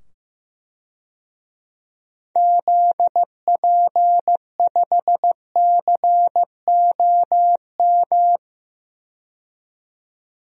Morse code, heard as a keyed tone, about 15 wpm, 700 Hz. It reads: ZP5COM
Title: DXCC-100 Call Signs - 6 Characters - 15wpm